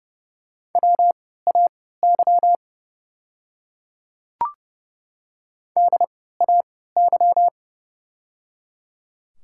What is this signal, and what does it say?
Morse code, heard as a keyed tone, about 30 wpm, spaced as for 17 wpm, 700 Hz. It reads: WAY DAY
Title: 3x Character Spacing - Top 100 Words - 30wpm